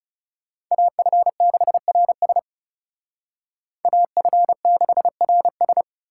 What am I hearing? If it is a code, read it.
Morse code, heard as a keyed tone, about 35 wpm, 700 Hz. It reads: AF6RS AF6RS
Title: Speed Racing - USA 2x2 Call Signs 30wpm